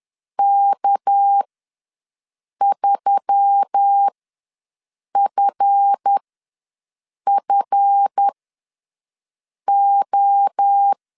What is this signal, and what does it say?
Morse code, heard as a keyed tone, about 10 wpm, spaced as for 5 wpm, 800 Hz. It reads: K3FFO
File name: LEZIONI / 7 GF9Z / 7 GF ws 10wpm 3918